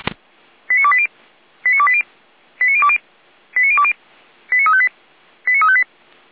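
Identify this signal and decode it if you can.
Five-tone selective call (ZVEI), heard as quick runs of five tones, tones of about 70 ms. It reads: 89280 89280 89020 89020 89358 89358
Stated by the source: ZVEI five-tone call received on SDR, feuerwehr-orlamuende.org fire brigade.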